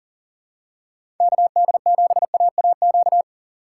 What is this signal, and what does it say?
Morse code, heard as a keyed tone, about 40 wpm, 700 Hz. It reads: KD7AAQ
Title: USA Call Signs - 6 Characters - 2x Character Spacing - 40wpm